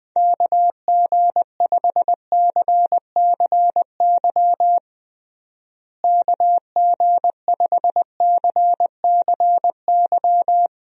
Morse code, heard as a keyed tone, about 20 wpm, 700 Hz. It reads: KG5CCY KG5CCY
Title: USA Call Signs - 6 Characters - Repeated 2x - 20wpm